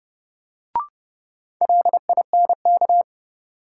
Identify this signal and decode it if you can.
Morse code, heard as a keyed tone, about 30 wpm, 700 Hz. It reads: LINK
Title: Sentences from Top 1500 Words - New Word Drill Alt - 30wpm